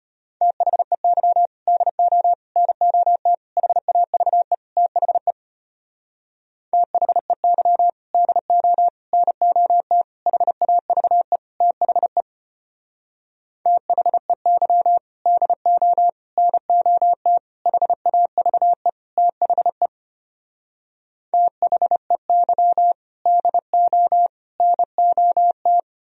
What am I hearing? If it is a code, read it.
Morse code, heard as a keyed tone, about 33 wpm, 700 Hz. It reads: THEY DO NOT HAVE THE THEY DO NOT HAVE THE THEY DO NOT HAVE THE THEY DO NOT